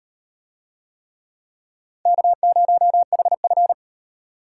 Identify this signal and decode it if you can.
Morse code, heard as a keyed tone, about 33 wpm, 700 Hz. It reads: K0HF